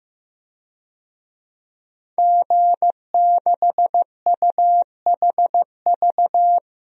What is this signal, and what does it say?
Morse code, heard as a keyed tone, about 16 wpm, 700 Hz. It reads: G6UHV
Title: DXCC-100 Call Signs - 5 Characters - Repeated 3x - 15wpm